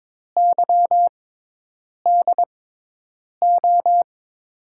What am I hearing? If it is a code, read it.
Morse code, heard as a keyed tone, about 21 wpm, spaced as for 8 wpm, 700 Hz. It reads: YDO
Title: USA Call Signs - 6 Characters - 6x Character Spacing - 22wpm